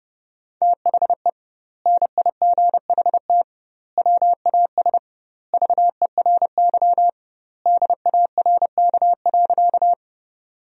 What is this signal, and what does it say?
Morse code, heard as a keyed tone, about 30 wpm, 700 Hz. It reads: THE NIGHT WAS VERY DARK.
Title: 2x Word-Spacing - Sentences from Top 500 Words - 30wpm